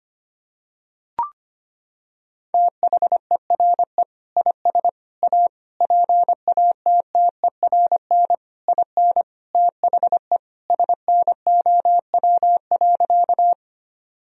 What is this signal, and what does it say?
Morse code, heard as a keyed tone, about 25 wpm, 700 Hz. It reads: THERE IS A PATTERN IN THE SNOW.
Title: Sentences from Top 1000 Words - Review Part 2 - 25wpm